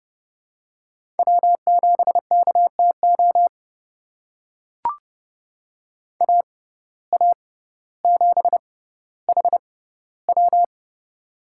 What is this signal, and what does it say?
Morse code, heard as a keyed tone, about 30 wpm, spaced as for 15 wpm, 700 Hz. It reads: W7KTO AA7HW
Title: USA Call Signs - 6x Character Spacing - 30wpm